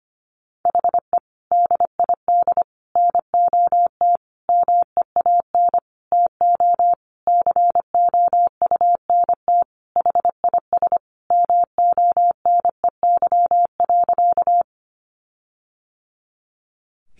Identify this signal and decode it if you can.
Morse code, heard as a keyed tone, about 25 wpm, 700 Hz. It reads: HE DID NOT MEAN TO COUNT HIS MONEY.
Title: Sentences from Top 1500 Words 25wpm